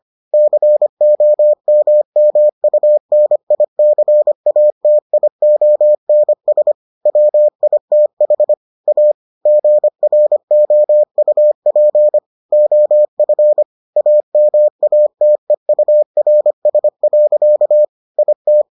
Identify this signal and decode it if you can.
Morse code, heard as a keyed tone, about 25 wpm, 600 Hz. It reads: COMMUNICATIONS WITH A GROUP OF AMATEURS. IT